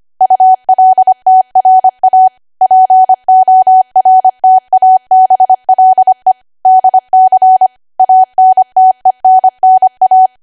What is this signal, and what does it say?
Morse code, heard as a keyed tone, about 25 wpm, 750 Hz. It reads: ULTRA PORTABLE DC ANTENNA
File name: Archive / 25 WPM / 25WPM